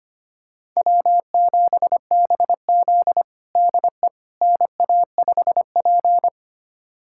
Morse code, heard as a keyed tone, about 25 wpm, 700 Hz. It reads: W7BZ DE NA5P
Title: USA Callsign DE Callsign - 4 Characters - 25wpm